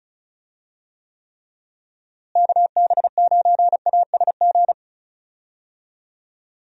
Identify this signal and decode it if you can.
Morse code, heard as a keyed tone, about 34 wpm, 700 Hz. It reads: KB9ASG
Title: USA Call Signs - 6 Characters - Repeated 3x - 35wpm